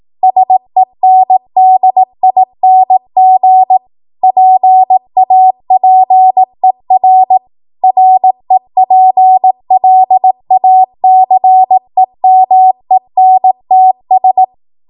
Morse code, heard as a keyed tone, about 18 wpm, 750 Hz. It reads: SENDING PAPER REPLACEMENTS
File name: Morse / 18WPM